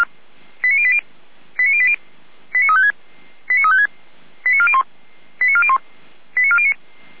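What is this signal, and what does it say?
Five-tone selective call (ZVEI), heard as quick runs of five tones, tones of about 70 ms. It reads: 89080 89080 89356 89356 89401 89401 89409
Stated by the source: ZVEI five-tone selective call received by SDR.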